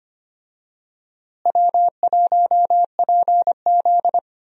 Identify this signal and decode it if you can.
Morse code, heard as a keyed tone, about 25 wpm, 700 Hz. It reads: W1PZ
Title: USA Call Signs - 4 Characters - Repeated 2x - 25wpm